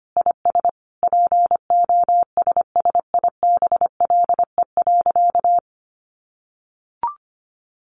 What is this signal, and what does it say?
Morse code, heard as a keyed tone, about 25 wpm, 700 Hz. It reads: IS POSSIBLE.
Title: Sentences from Top 900 Words 25wpm